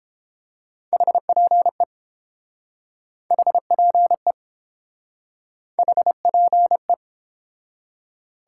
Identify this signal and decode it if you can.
Morse code, heard as a keyed tone, about 30 wpm, 700 Hz. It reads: HPE HPE HPE